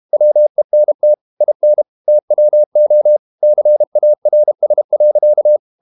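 Morse code, heard as a keyed tone, about 32 wpm, 600 Hz. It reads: WENT IN TWO CARS.
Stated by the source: Morse code practice at 32 wpm from CW med Gnister, reading Ken Follett's novel Never in English.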